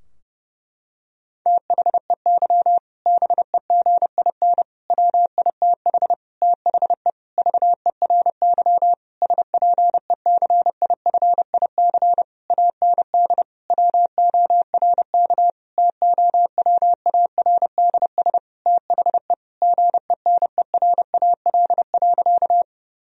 Morse code, heard as a keyed tone, about 30 wpm, 700 Hz. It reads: THEY BEGIN WITH THE VERY SPECIFIC AND WORK TOWARDS THE GENERAL.
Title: Sentences from Top 1000 Words - Review Part 3 - 30wpm